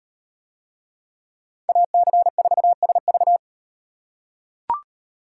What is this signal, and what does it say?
Morse code, heard as a keyed tone, about 38 wpm, 700 Hz. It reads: AC4SV